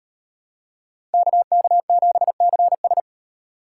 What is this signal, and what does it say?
Morse code, heard as a keyed tone, about 38 wpm, 700 Hz. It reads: KK7CS